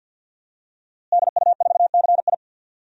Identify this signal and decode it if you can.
Morse code, heard as a keyed tone, about 50 wpm, 700 Hz. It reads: DU4CI
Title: DXCC-100 Call Signs - 5 Characters - Repeated 2x - 50wpm